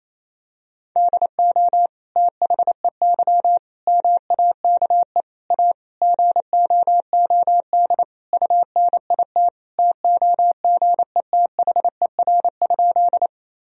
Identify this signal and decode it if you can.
Morse code, heard as a keyed tone, about 28 wpm, 700 Hz. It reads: DO THEY MAKE A GOOD UNIT TOGETHER?
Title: Sentences from Top 1000 Words - Review Part 2 - 28wpm